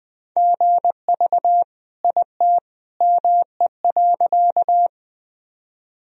Morse code, heard as a keyed tone, about 20 wpm, 700 Hz. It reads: GV IT ME.
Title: Sentences from Top 200 Words and Abbreviations 20wpm